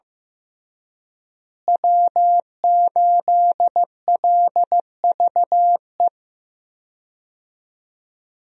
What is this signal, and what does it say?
Morse code, heard as a keyed tone, about 15 wpm, 700 Hz. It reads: W8LVE
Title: USA Call Signs - 5 Characters - Repeated 3x - 15wpm